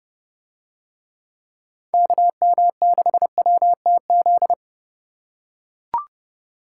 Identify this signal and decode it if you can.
Morse code, heard as a keyed tone, about 28 wpm, 700 Hz. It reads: KM6WTZ